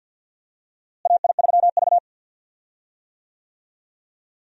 Morse code, heard as a keyed tone, about 53 wpm, 700 Hz. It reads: AI3V
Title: USA Call Signs - 4 Characters - Repeated 2x - 50wpm